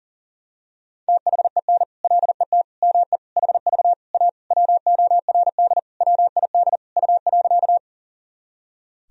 Morse code, heard as a keyed tone, about 37 wpm, 700 Hz. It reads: THEN LET ME HV A WORD WID U.
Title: Sentences from Top 300 Words and Abbreviations 40wpm